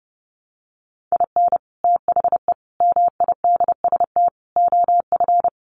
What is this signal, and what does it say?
Morse code, heard as a keyed tone, about 30 wpm, 700 Hz. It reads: IN THE MIDST OF
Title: Sets of 4 Words 30wpm